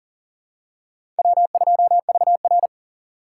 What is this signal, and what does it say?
Morse code, heard as a keyed tone, about 40 wpm, 700 Hz. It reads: W2VR
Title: USA Call Signs - 4 Characters - 6x Character Spacing - 40wpm